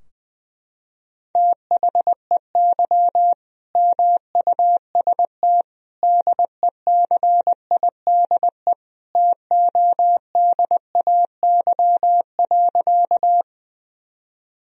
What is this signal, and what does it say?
Morse code, heard as a keyed tone, about 21 wpm, 700 Hz. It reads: THEY MUST DECIDE TODAY.
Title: Sentences from Top 1000 Words - Review Part 3 - 20wpm